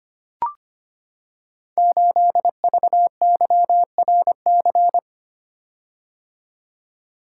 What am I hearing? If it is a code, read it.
Morse code, heard as a keyed tone, about 25 wpm, 700 Hz. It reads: 8VYRC
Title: Code Groups - 5 Letter:Numbers - 25wpm